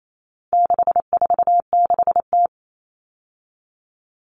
Morse code, heard as a keyed tone, about 28 wpm, 700 Hz. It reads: 646T
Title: All Cut Numbers - 4-Digit 28wpm